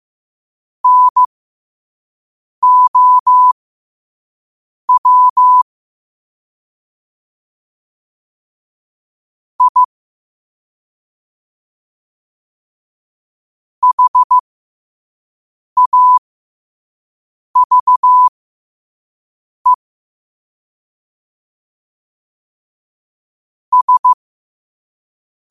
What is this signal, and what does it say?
Morse code, heard as a keyed tone, about 15 wpm, spaced as for 5 wpm, 1000 Hz. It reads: NOW I HAVE S
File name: 5wpm